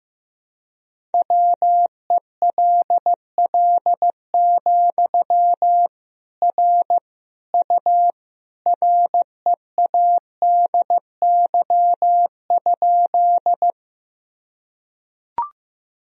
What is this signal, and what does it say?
Morse code, heard as a keyed tone, about 15 wpm, 700 Hz. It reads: WELL, R U READY?
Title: Sentences from Top 400 Words and Abbreviations 15wpm